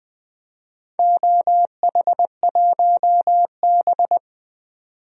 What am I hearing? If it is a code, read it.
Morse code, heard as a keyed tone, about 20 wpm, 700 Hz. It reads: OH1B
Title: DXCC-100 Call Signs - 4 Characters - 20wpm